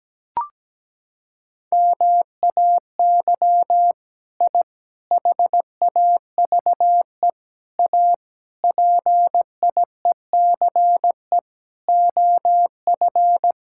Morse code, heard as a keyed tone, about 17 wpm, 700 Hz. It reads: MAY I HAVE A PIECE OF
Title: Sentences from Top 1000 Words - Review Part 5 - 17wpm